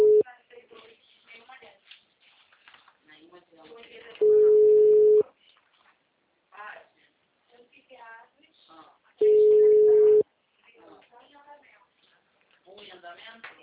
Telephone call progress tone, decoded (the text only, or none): ringback tone